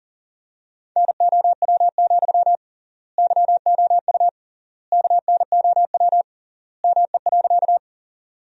Morse code, heard as a keyed tone, about 40 wpm, 700 Hz. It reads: NOW, YOU KNOW ME.
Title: 3x Word-Spacing - Sentences from Top 200 Words - 40wpm